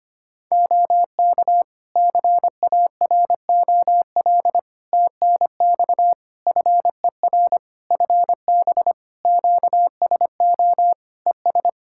Morse code, heard as a keyed tone, about 25 wpm, 700 Hz. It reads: OK CAROL TNX FER FB QSO ES
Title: K1USN SST QSO Practice 25wpm